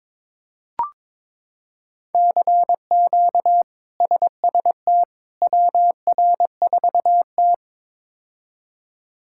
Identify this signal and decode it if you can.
Morse code, heard as a keyed tone, about 21 wpm, 700 Hz. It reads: CQ SST WR4T
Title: K1USN SST QSO Practice - 22wpm